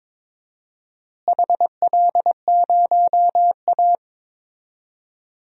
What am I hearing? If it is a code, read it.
Morse code, heard as a keyed tone, about 22 wpm, 700 Hz. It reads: HL0A
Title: DXCC-100 Call Signs - 4 Characters - Repeated 2x - 22wpm